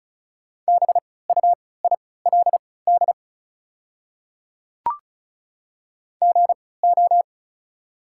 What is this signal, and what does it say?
Morse code, heard as a keyed tone, about 34 wpm, spaced as for 20 wpm, 700 Hz. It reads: BUILD GO
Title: 3x Character Spacing - Top 200 Words - 35wpm